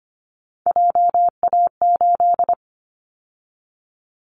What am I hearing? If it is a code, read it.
Morse code, heard as a keyed tone, about 25 wpm, 690 Hz. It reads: JA8
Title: Callsign Prefixes for Top 50 DXCC - 25wpm